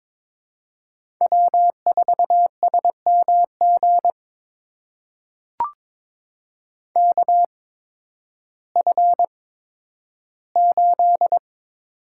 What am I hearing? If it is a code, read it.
Morse code, heard as a keyed tone, about 22 wpm, spaced as for 12 wpm, 700 Hz. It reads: W4SMG KF8
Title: USA Call Signs - 5 Characters - 8x Character Spacing - 22wpm